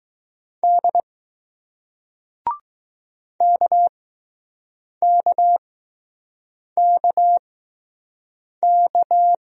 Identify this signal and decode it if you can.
Morse code, heard as a keyed tone, about 20 wpm, spaced as for 6 wpm, 700 Hz. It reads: D KKKK